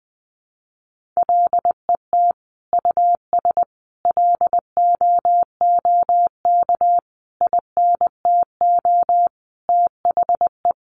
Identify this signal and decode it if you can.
Morse code, heard as a keyed tone, about 20 wpm, 700 Hz. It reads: LET US LOOK INTO THE
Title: Sentences from Top 1400 Words 20wpm